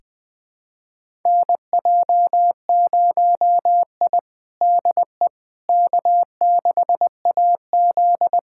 Morse code, heard as a keyed tone, about 20 wpm, 700 Hz. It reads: NJ0I DE K6AZ
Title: USA Callsign DE Callsign - 4 Characters - 20wpm